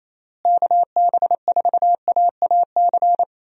Morse code, heard as a keyed tone, about 27 wpm, 700 Hz. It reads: KB4AAC
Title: USA Call Signs - 6 Characters - Repeated 2x - 28wpm